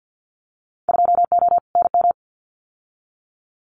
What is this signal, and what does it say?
Morse code, heard as a keyed tone, about 50 wpm, 700 Hz. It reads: 2O NM
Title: Winter Field Day Exchanges 50wpm